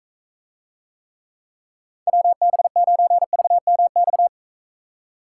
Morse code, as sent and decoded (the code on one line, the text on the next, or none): .-- -... ----. ...- -- -..-
WB9VMX